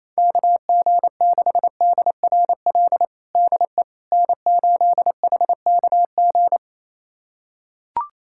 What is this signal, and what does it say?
Morse code, heard as a keyed tone, about 28 wpm, 700 Hz. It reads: KG6DRL DE N8HKG
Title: USA Callsign DE Callsign 28wpm